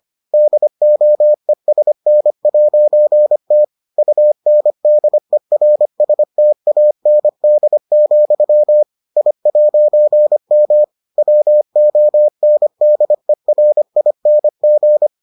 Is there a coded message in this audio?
Yes: Morse code from 0.3 to 15.1 s, heard as a keyed tone, about 25 wpm, 600 Hz. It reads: DOESN'T UNDERSTAND, I'M WONDERING